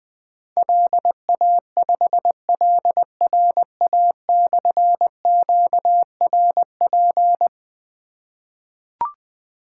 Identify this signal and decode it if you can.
Morse code, heard as a keyed tone, about 21 wpm, 700 Hz. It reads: LA5LRA/QRP
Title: Difficult Call Signs 20wpm